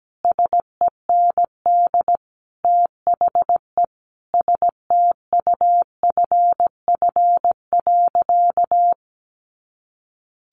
Morse code, heard as a keyed tone, about 17 wpm, 700 Hz. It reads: SEND THE STUFF.
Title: Sentences from Top 1500 Words 17wpm